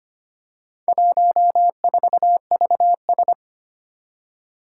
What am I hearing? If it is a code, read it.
Morse code, heard as a keyed tone, about 25 wpm, 700 Hz. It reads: 14VS
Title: Four Challenging Letter-Number Combinations 25wpm